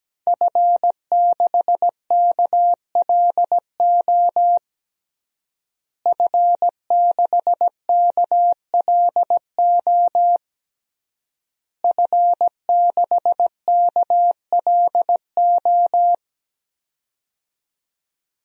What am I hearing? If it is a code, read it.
Morse code, heard as a keyed tone, about 17 wpm, 700 Hz. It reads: F6KLO F6KLO F6KLO
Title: DXCC-100 Call Signs - 5 Characters - Repeated 3x - 17wpm